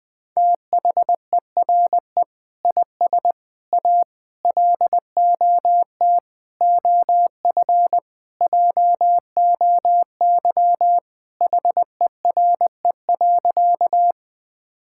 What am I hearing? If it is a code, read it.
Morse code, heard as a keyed tone, about 20 wpm, 700 Hz. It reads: THERE IS A LOT OF JOY HERE.
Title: Sentences with 5-Word Phrases - Spoken Beginning - 20wpm